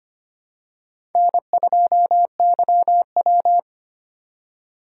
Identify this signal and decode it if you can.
Morse code, heard as a keyed tone, about 25 wpm, 700 Hz. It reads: N2YW
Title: USA Call Signs - 2x Character Spacing - 25wpm